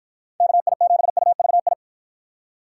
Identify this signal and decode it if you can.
Morse code, heard as a keyed tone, about 49 wpm, 700 Hz. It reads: KI7UVI